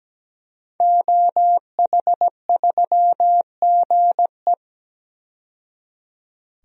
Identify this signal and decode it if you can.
Morse code, heard as a keyed tone, about 17 wpm, 700 Hz. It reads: OH3GE
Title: DXCC-100 Call Signs - 5 Characters - 17wpm